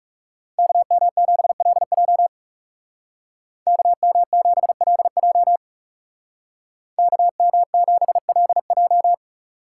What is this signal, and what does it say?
Morse code, heard as a keyed tone, about 36 wpm, 700 Hz. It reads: KM7LJ KM7LJ KM7LJ
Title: Speed Racing - USA 2x2 Call Signs 30wpm